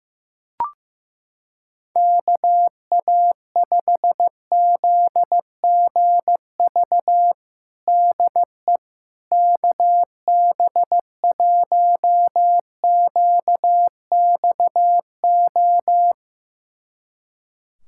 Morse code, heard as a keyed tone, about 15 wpm, 700 Hz. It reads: KA5ZGV DE KB1QXO
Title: USA Callsign DE Callsign - 6 Characters - 15wpm